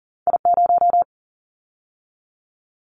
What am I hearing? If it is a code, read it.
Morse code, heard as a keyed tone, about 37 wpm, 700 Hz. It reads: I0
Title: Callsign Prefixes for Top 25 DXCC - 40wpm